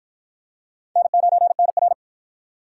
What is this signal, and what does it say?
Morse code, heard as a keyed tone, about 46 wpm, 700 Hz. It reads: N9NR